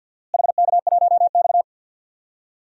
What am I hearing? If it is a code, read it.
Morse code, heard as a keyed tone, about 53 wpm, 700 Hz. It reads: HK1X
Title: DXCC-100 Call Signs - 4 Characters - Repeated 3x - 50wpm